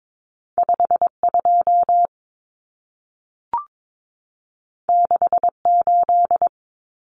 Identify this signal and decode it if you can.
Morse code, heard as a keyed tone, about 22 wpm, 700 Hz. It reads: 52 68
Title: Common Cut Numbers - 2-Digit 22wpm